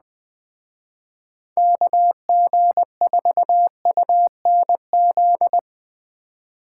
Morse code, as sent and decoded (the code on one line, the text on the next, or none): -.- --. ....- ..- -. --..
KG4UNZ